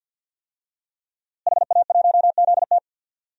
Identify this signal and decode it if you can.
Morse code, heard as a keyed tone, about 53 wpm, 700 Hz. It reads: HA1ZT
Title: DXCC-100 Call Signs - 5 Characters - Repeated 2x - 50wpm